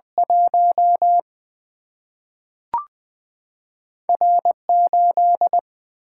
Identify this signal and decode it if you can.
Morse code, heard as a keyed tone, about 20 wpm, 700 Hz. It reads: 1 R8
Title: Single-Letter Single-Number 20wpm